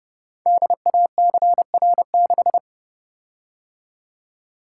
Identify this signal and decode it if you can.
Morse code, heard as a keyed tone, about 30 wpm, 700 Hz. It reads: DACR6
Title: Code Groups - 5 Letter:Numbers - 30wpm